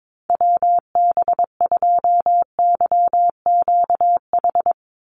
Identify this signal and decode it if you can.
Morse code, heard as a keyed tone, about 22 wpm, 690 Hz. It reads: WB2YQH
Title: Call Signs Repeated 2x - 22wpm